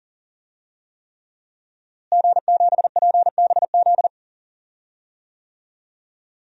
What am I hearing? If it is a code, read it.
Morse code, heard as a keyed tone, about 40 wpm, 700 Hz. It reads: G7PBZ
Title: DXCC-100 Call Signs - 5 Characters - Repeated 3x - 40wpm